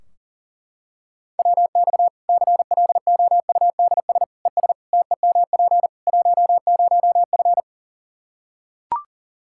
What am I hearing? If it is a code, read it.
Morse code, heard as a keyed tone, about 40 wpm, 700 Hz. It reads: WX CLOUDS ES TEMP 10F
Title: Practice QSO - Exchange 2 - Info - 40wpm